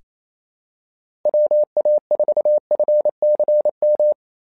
Morse code, heard as a keyed tone, about 28 wpm, 600 Hz. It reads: WA4FCM
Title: SOTA QSO Practice 28wpm